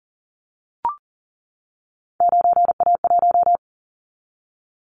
Morse code, heard as a keyed tone, about 37 wpm, 700 Hz. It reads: 9A1
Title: Callsign Prefixes for Top 50 DXCC - 40wpm